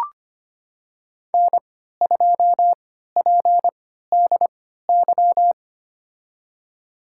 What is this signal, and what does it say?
Morse code, heard as a keyed tone, about 25 wpm, spaced as for 14 wpm, 700 Hz. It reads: N2PDY
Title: USA Call Signs - 5 Characters - 3x Character Spacing - 25wpm